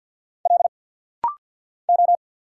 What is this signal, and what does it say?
Morse code, heard as a keyed tone, about 50 wpm, 690 Hz. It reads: L O
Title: Single Letter-Number Alt - Rapid-Fire 50wpm